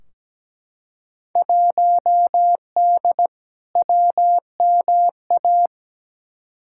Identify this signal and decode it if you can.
Morse code, heard as a keyed tone, about 17 wpm, 700 Hz. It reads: 1D WMA
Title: Field Day Exchanges - 17wpm